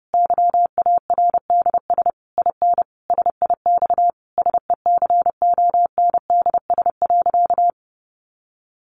Morse code, heard as a keyed tone, about 30 wpm, 700 Hz. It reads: YARDS IN SIX SECONDS.
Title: A Sound Of Thunder 30wpm